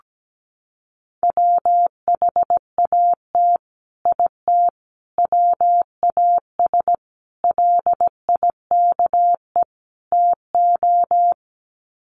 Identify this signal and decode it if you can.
Morse code, heard as a keyed tone, about 17 wpm, 700 Hz. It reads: WHAT IT WAS LIKE TO
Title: Sets of 5 Words Encore 17wpm